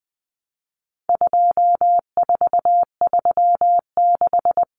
Morse code, heard as a keyed tone, about 20 wpm, 700 Hz. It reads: 2436
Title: Numbers - 4-Digit 20wpm